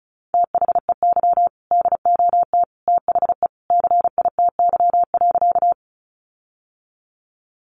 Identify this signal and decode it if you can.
Morse code, heard as a keyed tone, about 34 wpm, 700 Hz. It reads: THEY DOT THE CITY.
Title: Sentences from Top 1500 Words 35wpm